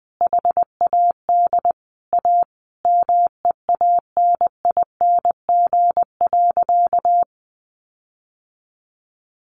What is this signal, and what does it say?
Morse code, heard as a keyed tone, about 20 wpm, 700 Hz. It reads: HAD A MEANING.
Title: Sentences from Top 1400 Words 20wpm